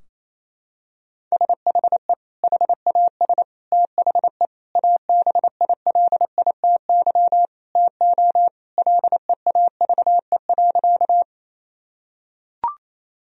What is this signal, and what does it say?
Morse code, heard as a keyed tone, about 28 wpm, 700 Hz. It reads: SHE HAS THE ABILITY TO LEAVE.
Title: Sentences from Top 1000 Words - Review Part 3 - 28wpm